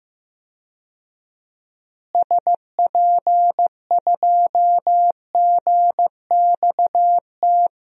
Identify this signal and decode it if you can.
Morse code, heard as a keyed tone, about 16 wpm, 700 Hz. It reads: SP2GXT
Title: DXCC-100 Call Signs - 6 Characters - Repeated 2x - 15wpm